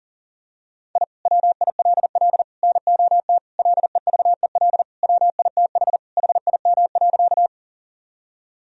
Morse code, heard as a keyed tone, about 40 wpm, 700 Hz. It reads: I WILL NOT LEVEL WITH HIM.
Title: Sentences from Top 1000 Words - Review Part 4 - 40wpm